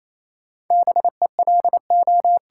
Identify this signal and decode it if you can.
Morse code, heard as a keyed tone, about 28 wpm, 700 Hz. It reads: BELO
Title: Sentences from Top 1200 Words - New Word Drill Alt - 28wpm